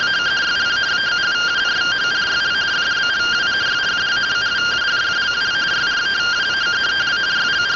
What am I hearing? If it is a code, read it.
Two-tone teleprinter signal, about 170 Hz shift.
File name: findings / axial iodine 10mm 184.gen-35 rtty rx